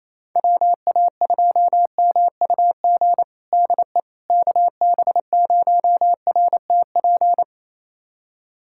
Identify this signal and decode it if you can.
Morse code, heard as a keyed tone, about 28 wpm, 700 Hz. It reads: WA2MUG DE KB0RTP
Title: USA Callsign DE Callsign - 6 Characters - 28wpm